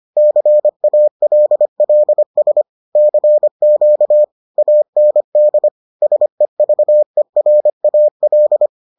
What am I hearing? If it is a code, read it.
Morse code, heard as a keyed tone, about 25 wpm, 600 Hz. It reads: CALLS CQ AND SEVERAL